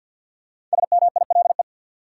Morse code, heard as a keyed tone, about 50 wpm, 700 Hz. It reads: SMILE
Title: Five Letter Words - 50wpm